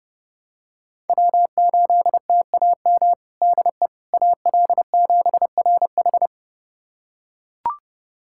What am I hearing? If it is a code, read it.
Morse code, heard as a keyed tone, about 31 wpm, 700 Hz. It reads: W8TAM DE AL7RH
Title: USA Callsign DE Callsign - 5 Characters - 30wpm